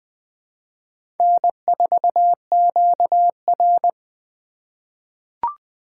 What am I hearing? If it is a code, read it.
Morse code, heard as a keyed tone, about 20 wpm, 700 Hz. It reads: N4QR
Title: USA Call Signs - 4 Characters - 3x Character Spacing - 20wpm